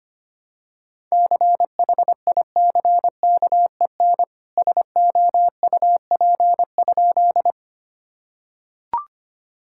Morse code, heard as a keyed tone, about 25 wpm, 700 Hz. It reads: CHICKEN SOUP?
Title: Sentences from Top 1000 Words - Review Part 6 - 25wpm